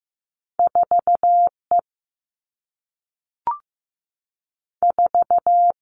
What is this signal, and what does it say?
Morse code, heard as a keyed tone, about 15 wpm, 700 Hz. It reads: 4 E 4
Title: All Cut Numbers - 2-Digit 15wpm